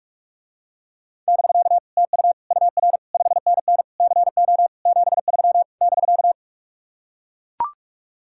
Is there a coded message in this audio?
Yes: Morse code from 1.3 to 6.3 s, heard as a keyed tone, about 49 wpm, 700 Hz. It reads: <BK> TU UR 5NN CO 73 <BK>